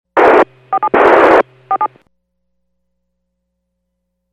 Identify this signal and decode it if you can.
DTMF, heard as telephone touch tones, tones of about 70 ms. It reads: 47 47